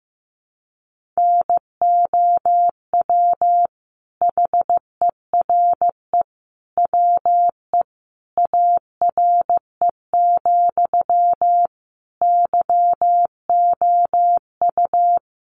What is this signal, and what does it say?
Morse code, heard as a keyed tone, about 15 wpm, 700 Hz. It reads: NOW HERE WE ARE, YOU
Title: Sentences from Top 1200 Words 15wpm